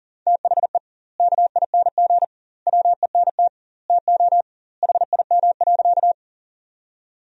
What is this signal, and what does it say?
Morse code, heard as a keyed tone, about 40 wpm, 700 Hz. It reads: THE KING WENT TO HIM.
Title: 2x Word-Spacing - Sentences from Top 400 Words - 40wpm